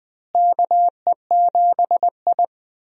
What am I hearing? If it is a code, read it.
Morse code, heard as a keyed tone, about 20 wpm, 700 Hz. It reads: KE7I
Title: USA Call Signs - 4 Characters - Repeated 2x - 20wpm